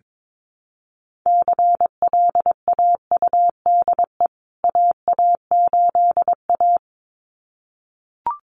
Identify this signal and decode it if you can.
Morse code, heard as a keyed tone, about 22 wpm, 700 Hz. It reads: CLAUDE AA8A
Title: CWT Exchange 22wpm